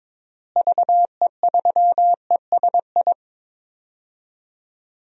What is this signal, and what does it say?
Morse code, heard as a keyed tone, about 22 wpm, 700 Hz. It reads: VE3ESI
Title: DXCC-100 Call Signs - 6 Characters - Repeated 2x - 22wpm